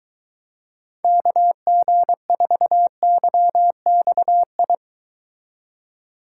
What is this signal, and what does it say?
Morse code, heard as a keyed tone, about 23 wpm, 700 Hz. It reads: KG4YXI